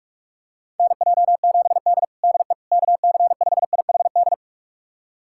Morse code, heard as a keyed tone, about 46 wpm, 700 Hz. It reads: NJ7D DE KC5IHD
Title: USA Callsign DE Callsign 45wpm